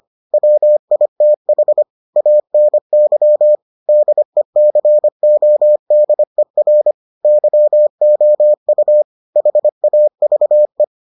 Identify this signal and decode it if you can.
Morse code, heard as a keyed tone, about 25 wpm, 600 Hz. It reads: WITH ANY DECODER YOU HAVE